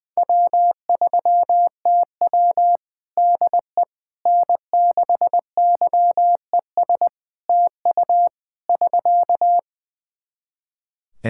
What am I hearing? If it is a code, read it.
Morse code, heard as a keyed tone, about 20 wpm, 700 Hz. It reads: W3TW DE N6YES TU <SK>
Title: Practice QSO - Exchange 4 - Ending - 20wpm